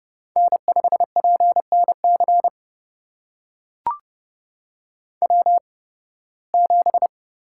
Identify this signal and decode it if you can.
Morse code, heard as a keyed tone, about 30 wpm, spaced as for 19 wpm, 700 Hz. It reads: N5PNC W7
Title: USA Call Signs - 5 Characters - 8x Character Spacing - 30wpm